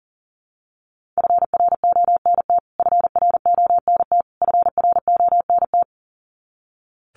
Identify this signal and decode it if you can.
Morse code, heard as a keyed tone, about 40 wpm, 700 Hz. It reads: FRONT FRONT FRONT
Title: Top 500 Words Repeated 40wpm